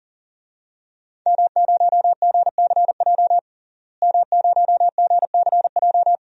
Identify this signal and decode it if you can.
Morse code, heard as a keyed tone, about 37 wpm, 700 Hz. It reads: M0GCJ M0GCJ
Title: DXCC-100 Call Signs - 5 Characters - Repeated 2x - 40wpm